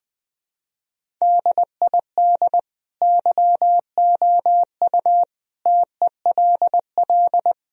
Morse code, heard as a keyed tone, about 20 wpm, 700 Hz. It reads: DID YOU TELL
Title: Sentences from Top 1000 Words - Review Part 4 - 20wpm